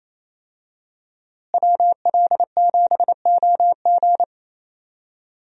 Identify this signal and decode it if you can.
Morse code, heard as a keyed tone, about 28 wpm, 700 Hz. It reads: WL7OG
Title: USA Call Signs - 5 Characters - Repeated 2x - 28wpm